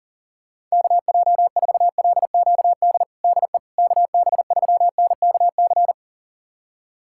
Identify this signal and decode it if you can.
Morse code, heard as a keyed tone, about 40 wpm, 700 Hz. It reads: KJ4LQD DE KB3NKC
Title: USA Callsign DE Callsign - 6 Characters - 40wpm